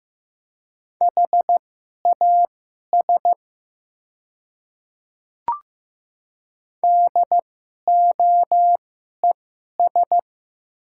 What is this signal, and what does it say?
Morse code, heard as a keyed tone, about 15 wpm, spaced as for 11 wpm, 700 Hz. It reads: HAS DOES
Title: 2x Character Spacing - Top 500 Words - 15wpm